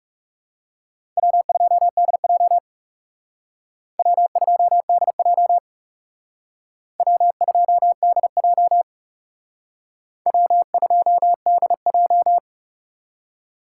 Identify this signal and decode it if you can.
Morse code, heard as a keyed tone, about 37 wpm, 700 Hz. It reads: W2DJ W2DJ W2DJ W2DJ
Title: Speed Racing - USA 1x2 Call Signs 30wpm